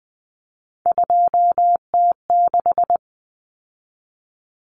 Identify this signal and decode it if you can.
Morse code, heard as a keyed tone, about 20 wpm, 700 Hz. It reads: 2T6
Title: Common Cut Numbers - 3-Digit 20wpm